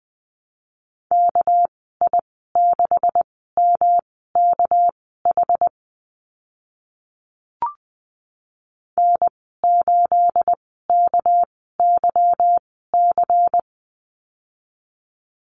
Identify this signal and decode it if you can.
Morse code, heard as a keyed tone, about 20 wpm, spaced as for 14 wpm, 700 Hz. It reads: KI6MKH N8KYC
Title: 245 Lesson 39g - Letter z - Call Signs 20wpm